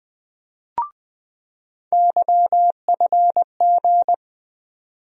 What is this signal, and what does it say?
Morse code, heard as a keyed tone, about 20 wpm, 700 Hz. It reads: YFG
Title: USA Call Sign Postfixes - 3 Letters 20wpm